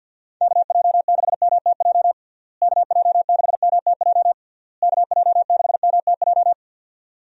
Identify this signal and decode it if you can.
Morse code, heard as a keyed tone, about 50 wpm, 700 Hz. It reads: KJ6MTJ KJ6MTJ KJ6MTJ
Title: USA Call Signs - 6 Characters - Repeated 3x - 50wpm